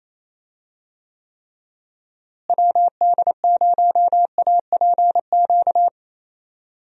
Morse code, heard as a keyed tone, about 27 wpm, 700 Hz. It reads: WD0APQ
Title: USA Call Signs - 6 Characters - Repeated 3x - 28wpm